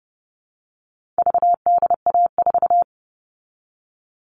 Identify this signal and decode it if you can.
Morse code, heard as a keyed tone, about 30 wpm, 700 Hz. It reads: VDA4
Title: All Cut Numbers - 4-Digit 30wpm